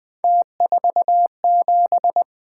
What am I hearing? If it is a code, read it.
Morse code, heard as a keyed tone, about 21 wpm, 700 Hz. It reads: T47
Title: Three Letter-Number Combinations - 20wpm